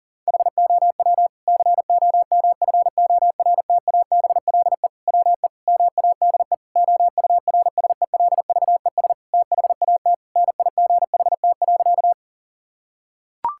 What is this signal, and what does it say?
Morse code, heard as a keyed tone, about 40 wpm, 700 Hz. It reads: HOW COMFORTABLE WE MADE OURSELVES THAT NIGHT.
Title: Sentences from Top 1600 Words 40wpm